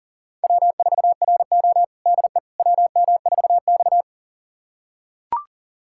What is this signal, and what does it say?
Morse code, heard as a keyed tone, about 40 wpm, 700 Hz. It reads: W4RO DE WM4X
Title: USA Callsign DE Callsign - 4 Characters - 40wpm